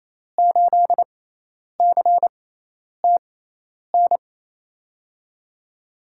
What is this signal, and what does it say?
Morse code, heard as a keyed tone, about 28 wpm, spaced as for 10 wpm, 700 Hz. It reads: 8CTN
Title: USA Call Signs - 6 Characters - 6x Character Spacing - 28wpm